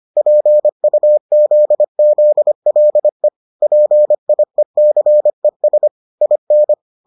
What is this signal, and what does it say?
Morse code, heard as a keyed tone, about 25 wpm, 600 Hz. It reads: PUZZLE PIECES IN